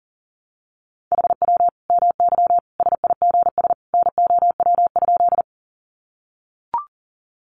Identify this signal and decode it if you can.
Morse code, heard as a keyed tone, about 40 wpm, 700 Hz. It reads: HW MY SIGS NOW?
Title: Common QSO Phrases 40wpm